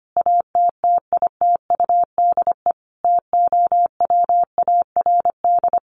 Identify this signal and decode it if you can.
Morse code, heard as a keyed tone, about 25 wpm, 700 Hz. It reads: ATTITUDE TOWARD
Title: Sentences from Top 1300 Words 25wpm